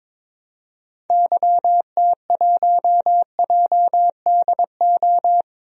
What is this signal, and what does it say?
Morse code, heard as a keyed tone, about 22 wpm, 700 Hz. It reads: YT1JDO
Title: DXCC-100 Call Signs - 6 Characters - 22wpm